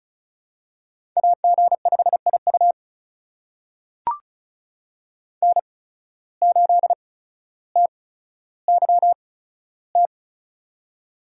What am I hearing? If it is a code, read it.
Morse code, heard as a keyed tone, about 34 wpm, spaced as for 15 wpm, 700 Hz. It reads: AG5IU N8TYT
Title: USA Call Signs - 5 Characters - 8x Character Spacing - 35wpm